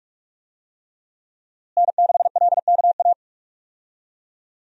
Morse code, heard as a keyed tone, about 46 wpm, 700 Hz. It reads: N6LKA
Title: USA Call Signs - 5 Characters - Repeated 3x - 45wpm